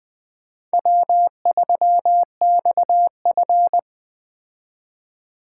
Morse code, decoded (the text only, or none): W3XF